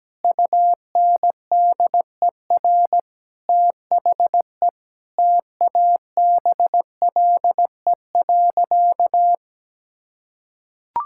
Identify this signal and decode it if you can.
Morse code, heard as a keyed tone, about 17 wpm, 700 Hz. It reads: UNDER THE TABLE.
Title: Sentences from Top 1000 Words - Review Part 5 - 17wpm